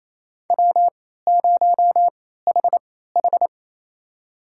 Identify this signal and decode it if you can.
Morse code, heard as a keyed tone, about 28 wpm, spaced as for 16 wpm, 700 Hz. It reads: W0HH
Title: USA Call Signs - 4 Characters - 3x Character Spacing - 28wpm